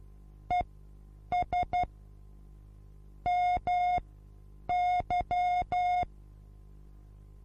Morse code, heard as a keyed tone, about 12 wpm, spaced as for 8 wpm, 720 Hz. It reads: ES MY
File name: repos / Morsekurs / DJ4UF Morsen23